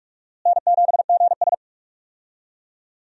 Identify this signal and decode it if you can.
Morse code, heard as a keyed tone, about 44 wpm, 700 Hz. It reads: N7GS
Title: USA Call Signs - 4 Characters - Repeated 2x - 45wpm